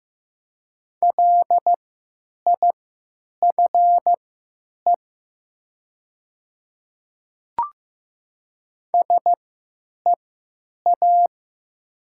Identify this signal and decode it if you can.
Morse code, heard as a keyed tone, about 15 wpm, spaced as for 9 wpm, 700 Hz. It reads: LIFE SEA
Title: 3x Character Spacing - Top 500 Words - 15wpm